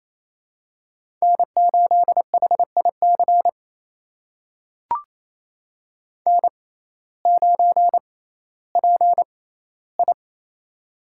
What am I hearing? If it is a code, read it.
Morse code, heard as a keyed tone, about 27 wpm, spaced as for 15 wpm, 700 Hz. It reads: N8HIC N9PI
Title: USA Call Signs - 6x Character Spacing - 28wpm